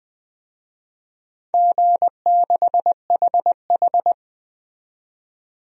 Morse code, heard as a keyed tone, about 20 wpm, 700 Hz. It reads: G6HH
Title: DXCC-100 Call Signs - 4 Characters - 20wpm